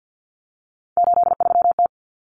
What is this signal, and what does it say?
Morse code, heard as a keyed tone, about 50 wpm, 700 Hz. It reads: 83T
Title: Common Cut Numbers - 3-Digit 50wpm